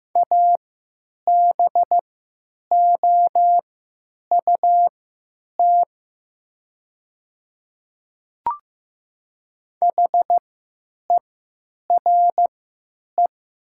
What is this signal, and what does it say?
Morse code, heard as a keyed tone, about 15 wpm, spaced as for 9 wpm, 700 Hz. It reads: ABOUT HERE
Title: 3x Character Spacing - Top 500 Words - 15wpm